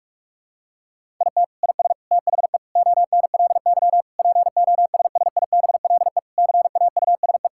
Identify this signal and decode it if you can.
Morse code, heard as a keyed tone, about 46 wpm, 700 Hz. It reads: IT IS THE ONLY POSSIBLE CAUSE
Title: Sentences from Top 1600 Words 45wpm